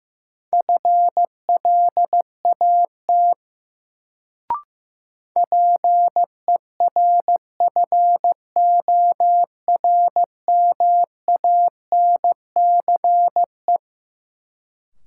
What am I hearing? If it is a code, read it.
Morse code, heard as a keyed tone, about 15 wpm, 700 Hz. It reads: FLAT PERFORMANCE
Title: Sentences from Top 1000 Words - New Word Drill - 15wpm